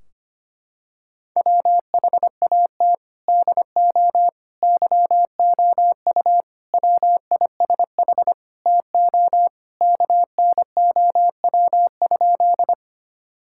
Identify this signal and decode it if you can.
Morse code, heard as a keyed tone, about 25 wpm, 700 Hz. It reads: WHAT DO YOU WISH TO KNOW?
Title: Sentences from Top 1600 Words 25wpm